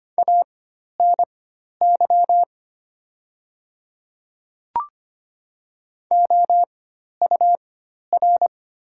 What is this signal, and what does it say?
Morse code, heard as a keyed tone, about 25 wpm, spaced as for 12 wpm, 700 Hz. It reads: ANY OUR
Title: 4x Character Spacing - Top 200 Words - 25wpm